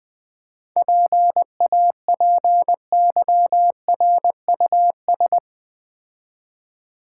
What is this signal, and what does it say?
Morse code, heard as a keyed tone, about 21 wpm, 700 Hz. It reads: PAPYRUS
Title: Easy Words - Speed-Racing with Hints - 20wpm